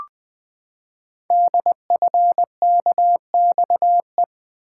Morse code, heard as a keyed tone, about 20 wpm, 700 Hz. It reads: DFKXE
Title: Code Groups - 5 Letters - 20wpm